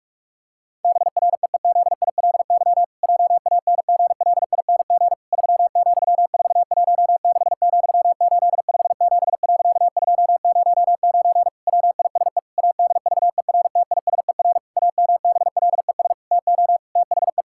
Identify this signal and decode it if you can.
Morse code, heard as a keyed tone, about 46 wpm, 700 Hz. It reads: BREEZILY JANGLING 3,416,857.209 WISE ADVERTISER AMBLES TO THE
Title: Pangram Challenge v2 - 45wpm